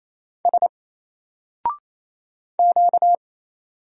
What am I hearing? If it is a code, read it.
Morse code, heard as a keyed tone, about 27 wpm, 700 Hz. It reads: S Q
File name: Single-Letter-Number-28wpm